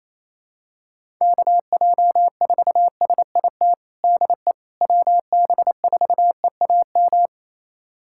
Morse code, heard as a keyed tone, about 28 wpm, 700 Hz. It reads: KJ4SIT DE WB4EAM
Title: USA Callsign DE Callsign - 6 Characters - 28wpm